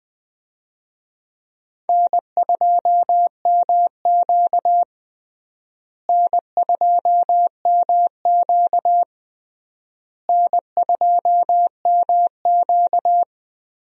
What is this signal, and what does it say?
Morse code, heard as a keyed tone, about 20 wpm, 700 Hz. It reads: N2MQ N2MQ N2MQ
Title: USA Call Signs - 4 Characters - Repeated 3x - 20wpm